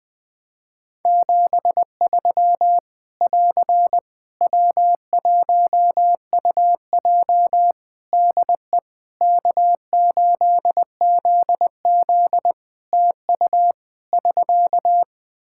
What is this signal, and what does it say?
Morse code, heard as a keyed tone, about 20 wpm, 700 Hz. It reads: 73 <AR> W1UJ DE K8ZZ TU <SK>
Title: Practice QSO - Exchange 4 - Ending - 20wpm